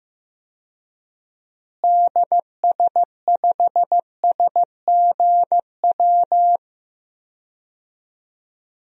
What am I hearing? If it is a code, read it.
Morse code, heard as a keyed tone, about 15 wpm, 700 Hz. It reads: DS5SGW
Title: DXCC-100 Call Signs - 6 Characters - Repeated 3x - 15wpm